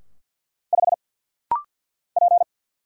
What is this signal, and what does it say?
Morse code, heard as a keyed tone, about 49 wpm, 700 Hz. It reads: 5 P
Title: Single Letter-Number - Rapid-Fire 50wpm